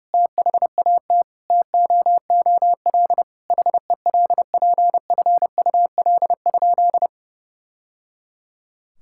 Morse code, heard as a keyed tone, about 30 wpm, 700 Hz. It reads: THAT TOOL HELPFUL?
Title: Sentences from Top 1000 Words - Review Part 6 - 30wpm